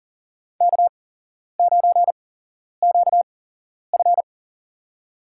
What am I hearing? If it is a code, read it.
Morse code, heard as a keyed tone, about 40 wpm, spaced as for 11 wpm, 700 Hz. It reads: K9QF
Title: USA Call Signs - 4 Characters - 8x Character Spacing - 40wpm